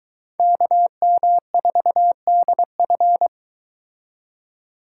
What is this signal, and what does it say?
Morse code, heard as a keyed tone, about 22 wpm, 700 Hz. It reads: KM4DF